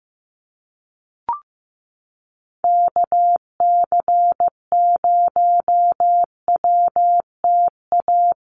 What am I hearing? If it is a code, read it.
Morse code, heard as a keyed tone, about 15 wpm, 700 Hz. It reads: KC0WTA
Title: USA Call Signs Repeated 3x 15wpm